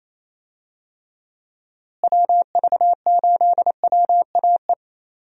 Morse code, heard as a keyed tone, about 28 wpm, 700 Hz. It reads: WV8WAE
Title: USA Call Signs - 6 Characters - Repeated 2x - 28wpm